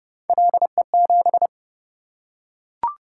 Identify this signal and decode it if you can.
Morse code, heard as a keyed tone, about 25 wpm, 700 Hz. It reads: LE7